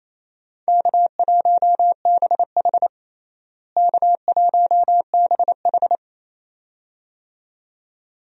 Morse code, heard as a keyed tone, about 27 wpm, 700 Hz. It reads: K1BH K1BH
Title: USA Call Signs - 4 Characters - Repeated 2x - 28wpm